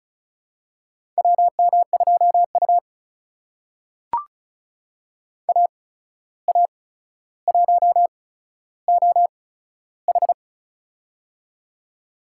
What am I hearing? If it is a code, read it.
Morse code, heard as a keyed tone, about 36 wpm, spaced as for 14 wpm, 700 Hz. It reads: WM2U AA1OH
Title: USA Call Signs - 8x Character Spacing - 35wpm